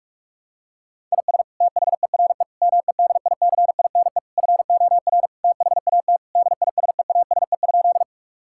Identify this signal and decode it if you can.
Morse code, heard as a keyed tone, about 42 wpm, 700 Hz. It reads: IS THERE MEDICINE FOR THAT DISEASE?
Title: Sentences from Top 1000 Words - Review Part 6 - 45wpm